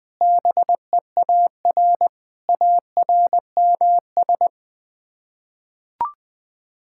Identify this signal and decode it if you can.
Morse code, heard as a keyed tone, about 20 wpm, 700 Hz. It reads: BEAR ARMS
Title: Sets of 9 Words - 20wpm